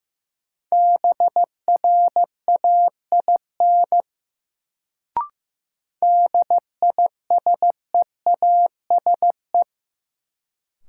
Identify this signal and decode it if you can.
Morse code, heard as a keyed tone, about 15 wpm, 700 Hz. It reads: BRAIN DISEASE
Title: Sentences from Top 1000 Words - Review Part 6 - New Word Drill - 15wpm